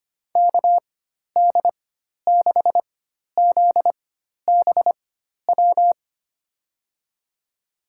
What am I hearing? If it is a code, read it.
Morse code, heard as a keyed tone, about 25 wpm, spaced as for 12 wpm, 700 Hz. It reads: KD6ZBW
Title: USA Call Signs - 4x Character Spacing - 25wpm